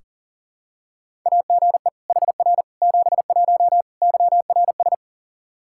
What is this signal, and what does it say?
Morse code, heard as a keyed tone, about 40 wpm, 700 Hz. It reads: AGE HR 71 YRS
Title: Practice QSO - Exchange 3 - More Info - 40wpm